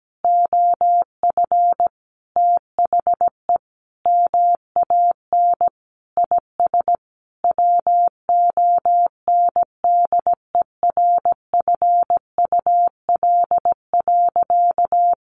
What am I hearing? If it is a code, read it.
Morse code, heard as a keyed tone, about 17 wpm, 700 Hz. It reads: OF THE MAN IS WONDERFUL.
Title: Sentences from Top 1400 Words 17wpm